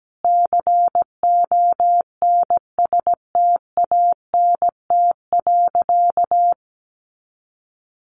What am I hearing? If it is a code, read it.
Morse code, heard as a keyed tone, about 17 wpm, 700 Hz. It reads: CONSTANT.
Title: Sentences from Top 1500 Words 17wpm